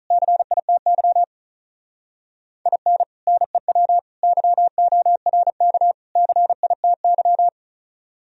Morse code, heard as a keyed tone, about 42 wpm, 700 Hz. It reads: CITY IN NEW YORK CITY
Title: Speed-Racing - Sets of 4 Words 35wpm